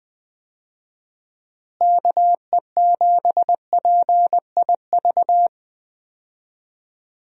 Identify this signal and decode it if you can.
Morse code, heard as a keyed tone, about 20 wpm, 700 Hz. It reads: KE7PIV
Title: USA Call Signs - 6 Characters - Repeated 3x - 20wpm